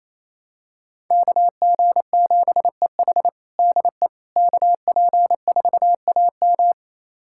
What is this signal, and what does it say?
Morse code, heard as a keyed tone, about 28 wpm, 700 Hz. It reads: KG7EH DE KP4AM
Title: USA Callsign DE Callsign - 5 Characters - 28wpm